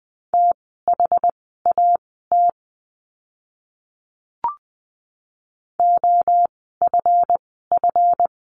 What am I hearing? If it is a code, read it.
Morse code, heard as a keyed tone, about 20 wpm, spaced as for 15 wpm, 700 Hz. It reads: THAT OFF
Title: 244 Lesson 39f - Letter z - Words 20wpm